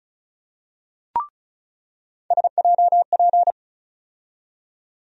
Morse code, heard as a keyed tone, about 34 wpm, 700 Hz. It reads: SJP
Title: USA Call Sign Postfixes - 3 Letters 35wpm